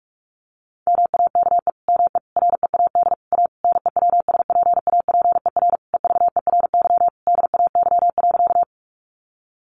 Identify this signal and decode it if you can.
Morse code, heard as a keyed tone, about 44 wpm, 700 Hz. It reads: MAKE ME READ A NEWSPAPER EVERY DAY.
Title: Sentences from Top 1100 Words 45wpm